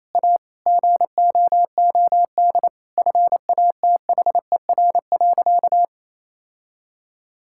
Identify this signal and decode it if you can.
Morse code, heard as a keyed tone, about 28 wpm, 700 Hz. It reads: A GOOD FATHER.
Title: Sentences from Top 1000 Words - Review Part 5 - 28wpm